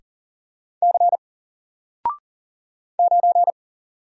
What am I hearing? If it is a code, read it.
Morse code, heard as a keyed tone, about 40 wpm, 700 Hz. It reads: C 9
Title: Single Letter-Number Alt 40wpm